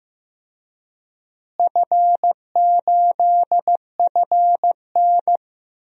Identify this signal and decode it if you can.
Morse code, heard as a keyed tone, about 15 wpm, 700 Hz. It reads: F8FN
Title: DXCC-100 Call Signs - 4 Characters - Repeated 2x - 15wpm